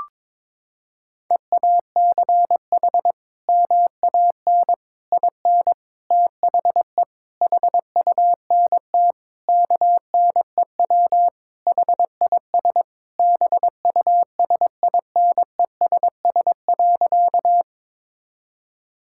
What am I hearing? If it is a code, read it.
Morse code, heard as a keyed tone, about 22 wpm, 700 Hz. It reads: EACH MAN IN THE HUNT KNEW HIS BUSINESS.
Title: Sentences from Top 1600 Words 22wpm